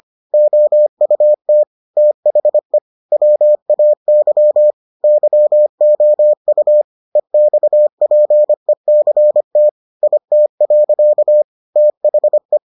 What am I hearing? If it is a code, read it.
Morse code, heard as a keyed tone, about 25 wpm, 600 Hz. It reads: OUT THE WAY YOU EXPECT IT. THE